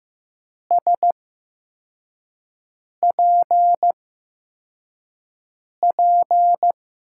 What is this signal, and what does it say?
Morse code, heard as a keyed tone, about 15 wpm, 700 Hz. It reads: S P P
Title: USA Call Signs - 5 Characters - 8x Character Spacing - 15wpm